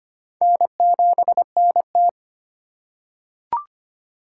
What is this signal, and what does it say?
Morse code, heard as a keyed tone, about 25 wpm, 700 Hz. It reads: N7NT
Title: K1USN SST QSO Practice - 25wpm